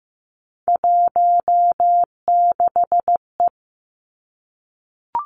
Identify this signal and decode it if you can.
Morse code, heard as a keyed tone, about 15 wpm, 700 Hz. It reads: 16E
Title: Three Challenging Letter-Number Combinations 15wpm